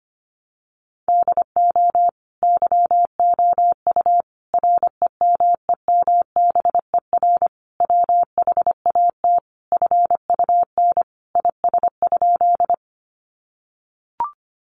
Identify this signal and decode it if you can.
Morse code, heard as a keyed tone, about 25 wpm, 700 Hz. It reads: DO YOU REMEMBER WHAT FUN IS?
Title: Sentences from Top 700 Words 25wpm